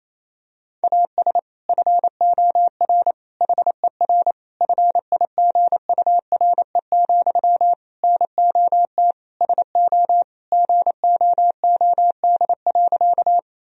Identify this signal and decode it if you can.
Morse code, heard as a keyed tone, about 28 wpm, 700 Hz. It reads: AS FOR HER FIGURE, NOT SO GOOD.
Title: Sentences from Top 1000 Words - Review Part 2 - 28wpm